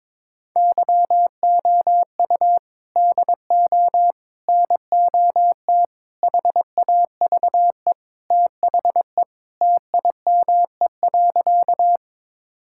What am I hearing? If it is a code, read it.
Morse code, heard as a keyed tone, about 22 wpm, 700 Hz. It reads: YOU DO NOT HAVE THE TIME.
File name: practice / 22wpm / Sentences with 4-Word Phrases - 22wpm